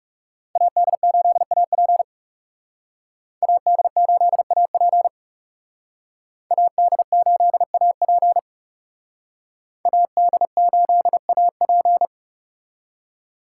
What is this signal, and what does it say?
Morse code, heard as a keyed tone, about 34 wpm, 700 Hz. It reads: AD8AP AD8AP AD8AP AD8AP
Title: Speed Racing - USA 2x2 Call Signs 30wpm